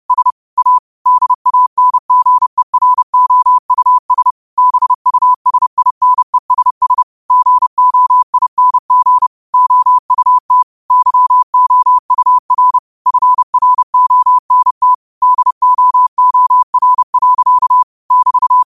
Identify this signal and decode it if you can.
Morse code, heard as a keyed tone, about 30 wpm, 1000 Hz. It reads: S A DANGEROUS BUSINESS GOING OUT YOUR FRONT DOOR. <BT>